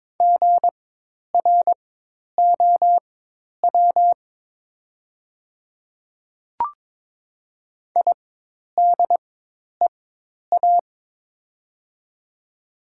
Morse code, heard as a keyed tone, about 22 wpm, spaced as for 10 wpm, 700 Hz. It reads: GROW IDEA
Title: 4x Character Spacing - Top 500 Words - 22wpm